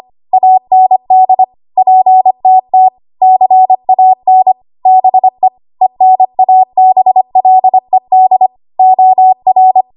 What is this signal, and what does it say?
Morse code, heard as a keyed tone, about 25 wpm, 750 Hz. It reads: AND PTT CAN BE ENABLED OR
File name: Archive / 25 WPM / 25WPM